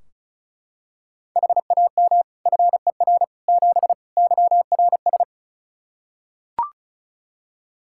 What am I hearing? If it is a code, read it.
Morse code, heard as a keyed tone, about 36 wpm, 700 Hz. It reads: HAM FER 7 YRS
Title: Practice QSO - Exchange 3 - More Info - 35wpm